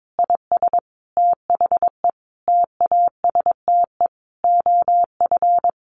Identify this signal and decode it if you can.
Morse code, heard as a keyed tone, about 22 wpm, 700 Hz. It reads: IS THE TASTE OF
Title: Sentences from Top 1200 Words 22wpm